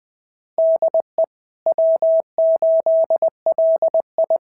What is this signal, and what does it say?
Morse code, heard as a keyed tone, about 20 wpm, 660 Hz. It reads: DE W8LI
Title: SOTA QSO Practice 20wpm